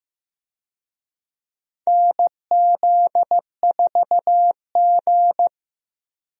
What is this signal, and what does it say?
Morse code, heard as a keyed tone, about 15 wpm, 700 Hz. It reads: NZ4G
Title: USA Call Signs - 4 Characters - 15wpm